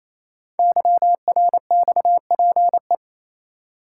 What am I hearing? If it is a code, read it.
Morse code, heard as a keyed tone, about 27 wpm, 700 Hz. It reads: YRXPE
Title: Code Groups - 5 Letters - 28wpm